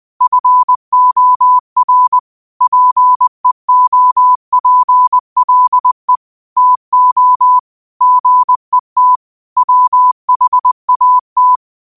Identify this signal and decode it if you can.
Morse code, heard as a keyed tone, about 20 wpm, 1000 Hz. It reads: FOR PEOPLE TO GET WHAT
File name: qotdfiles / 20wpm